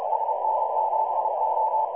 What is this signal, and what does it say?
Morse code, heard as a keyed tone, about 31 wpm, 880 Hz, with noise. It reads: SN7O
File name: pub / spdxc 2018 / NA5B websdr recording 14034